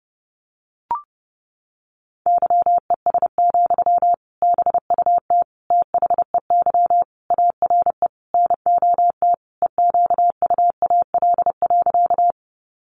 Morse code, heard as a keyed tone, about 30 wpm, 700 Hz. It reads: YES, BUT THEY ARE NOT EQUAL.
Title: Sentences from Top 1400 Words 30wpm